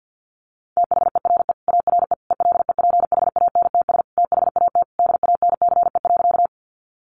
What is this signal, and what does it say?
Morse code, heard as a keyed tone, about 50 wpm, 700 Hz. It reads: THERE ARE ELEPHANTS THAT DANCE.
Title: Easy Phrases 50wpm